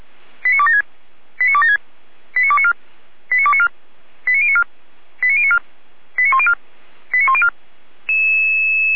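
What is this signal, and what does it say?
Five-tone selective call (ZVEI), heard as quick runs of five tones, tones of about 70 ms. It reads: 89276 89276 89294 89294 89094 89094 89104 89104
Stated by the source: ZVEI five-tone selective call received on an SDR.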